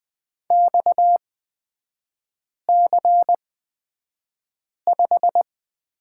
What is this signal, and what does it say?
Morse code, heard as a keyed tone, about 20 wpm, spaced as for 5 wpm, 700 Hz. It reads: XC5
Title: 236 Lesson 38e - Letter q - Three Characters 20wpm